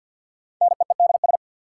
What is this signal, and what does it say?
Morse code, heard as a keyed tone, about 53 wpm, 700 Hz. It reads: NEEDS
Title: Sentences from Top 1000 Words - Review Part 3 - New Word Drill - 50wpm